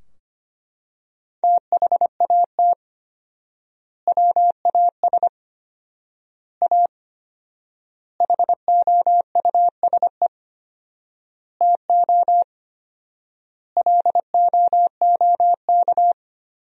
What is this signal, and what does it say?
Morse code, heard as a keyed tone, about 25 wpm, 700 Hz. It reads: THAT WAS A HOUSE TO LOOK
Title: 4x Word-Spacing - Sentences from Top 200 Words - 25wpm